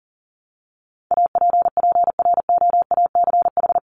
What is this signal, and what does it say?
Morse code, heard as a keyed tone, about 40 wpm, 700 Hz. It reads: APPROACH
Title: Top 751-1000 Words 40wpm